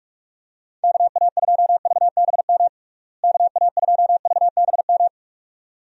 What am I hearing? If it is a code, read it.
Morse code, heard as a keyed tone, about 46 wpm, 700 Hz. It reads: KA2VBM KA2VBM
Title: USA Call Signs - 6 Characters - Repeated 2x - 45wpm